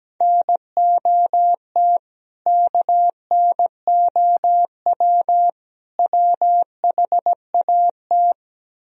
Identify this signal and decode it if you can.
Morse code, heard as a keyed tone, about 17 wpm, 700 Hz. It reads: NOT KNOW WHAT
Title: Speed-Racing - Sets of 3 Words 17wpm